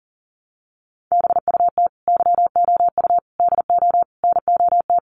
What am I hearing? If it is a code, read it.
Morse code, heard as a keyed tone, about 40 wpm, 700 Hz. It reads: BUT YOU DO NOT
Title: Sets of 5 Words Encore 40wpm